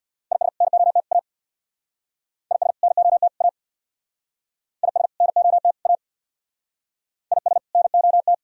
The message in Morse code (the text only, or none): IS NOT A IS NOT A IS NOT A IS NOT